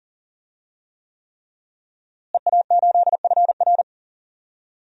Morse code, heard as a keyed tone, about 40 wpm, 700 Hz. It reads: EA8FR
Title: DXCC-100 Call Signs - 5 Characters - Repeated 3x - 40wpm